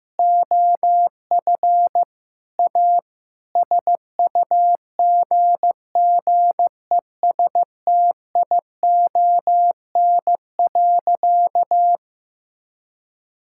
Morse code, heard as a keyed tone, about 15 wpm, 700 Hz. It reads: OF A SUGGESTION.
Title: Sentences from Top 1600 Words 15wpm